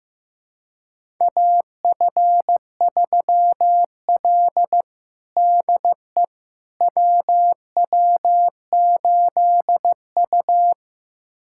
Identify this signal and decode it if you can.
Morse code, heard as a keyed tone, about 15 wpm, 700 Hz. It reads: AF3L DE WW8U
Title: USA Callsign DE Callsign - 4 Characters - 15wpm